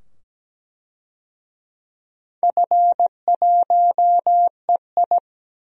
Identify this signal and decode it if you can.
Morse code, heard as a keyed tone, about 17 wpm, 700 Hz. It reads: F1EI
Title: DXCC-100 Call Signs - 4 Characters - 17wpm